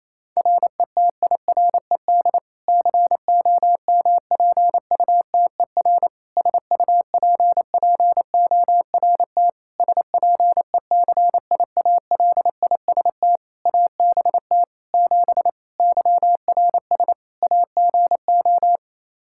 Morse code, heard as a keyed tone, about 28 wpm, 700 Hz. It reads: RETIRED COMPUTER SUPPORT SPECIALIST ABT 7 YRS AGO
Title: Practice QSO - Exchange 3 - More Info - 28wpm